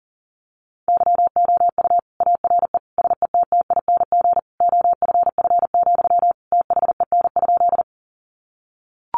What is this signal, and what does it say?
Morse code, heard as a keyed tone, about 40 wpm, 700 Hz. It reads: YOU ARE SETTING OFF, THEN?
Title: Sentences from Top 1300 Words 40wpm